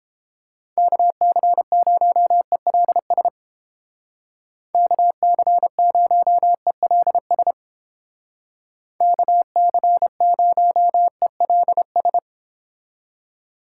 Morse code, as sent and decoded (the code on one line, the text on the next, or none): -.- -.-. ----- . .-.. ... / -.- -.-. ----- . .-.. ... / -.- -.-. ----- . .-.. ...
KC0ELS KC0ELS KC0ELS